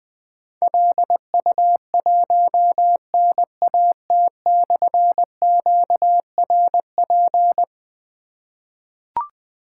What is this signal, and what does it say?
Morse code, heard as a keyed tone, about 20 wpm, 700 Hz. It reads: LU1NAT/QRP
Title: Difficult Call Signs 20wpm